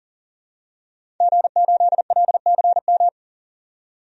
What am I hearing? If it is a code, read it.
Morse code, heard as a keyed tone, about 40 wpm, 700 Hz. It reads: G8LCM
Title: DXCC-100 Call Signs - 5 Characters - 40wpm